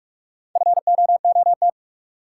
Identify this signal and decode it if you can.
Morse code, heard as a keyed tone, about 44 wpm, 700 Hz. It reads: FOOT
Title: Sets of 5 Words - New Word Drill - 45wpm